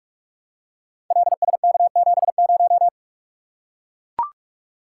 Morse code, heard as a keyed tone, about 44 wpm, 700 Hz. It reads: LSK70
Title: Code Groups - 5 Letter:Numbers - 45wpm